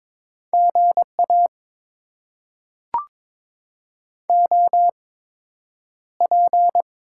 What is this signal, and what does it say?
Morse code, heard as a keyed tone, about 22 wpm, spaced as for 9 wpm, 700 Hz. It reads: GA OP
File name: Top 90 QSO Elements - 8x Character Spacing - 22wpm